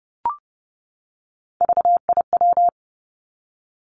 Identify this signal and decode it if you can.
Morse code, heard as a keyed tone, about 30 wpm, 700 Hz. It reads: VIW
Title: Three Challenging Letter-Number Combinations 30wpm